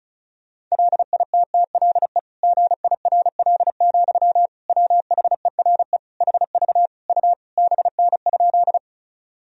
Morse code, heard as a keyed tone, about 34 wpm, 700 Hz. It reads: LITTLE GIRL, WHERE HV U BN?
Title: Sentences from Top 400 Words and Abbreviations 35wpm